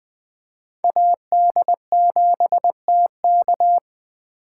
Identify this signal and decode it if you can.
Morse code, heard as a keyed tone, about 20 wpm, 700 Hz. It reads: AD7TK
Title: USA Call Signs - 5 Characters - 2x Character Spacing - 20wpm